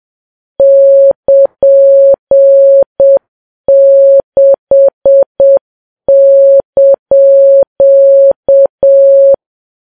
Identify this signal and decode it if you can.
Morse code, heard as a keyed tone, about 7 wpm, 550 Hz. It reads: <KN>6)
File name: Files Web tls / Morse on9fv / morse7kps